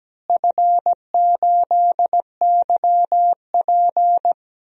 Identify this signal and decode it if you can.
Morse code, heard as a keyed tone, about 17 wpm, 700 Hz. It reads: F8YP
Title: DXCC-100 Call Signs - 4 Characters - Repeated 2x - 17wpm